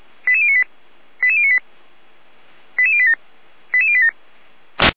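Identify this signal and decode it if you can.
Five-tone selective call (ZVEI), heard as quick runs of five tones, tones of about 70 ms. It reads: 88098 88098 88087 88087
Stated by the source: ZVEI five-tone selective call received by SDR.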